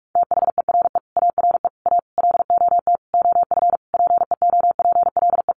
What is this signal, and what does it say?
Morse code, heard as a keyed tone, about 44 wpm, 700 Hz. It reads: THERE ARE A LOT OF PEOPLE
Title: Easy Phrases 45wpm